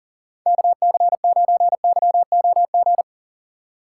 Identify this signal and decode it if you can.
Morse code, heard as a keyed tone, about 40 wpm, 700 Hz. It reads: KC9YOG
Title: USA Call Signs - 6 Characters - 3x Character Spacing - 40wpm